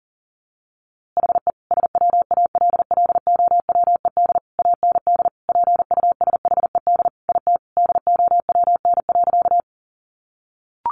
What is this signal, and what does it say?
Morse code, heard as a keyed tone, about 40 wpm, 700 Hz. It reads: HE SWALLOWED AND PUSHED IT DOWN.
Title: A Sound Of Thunder 40wpm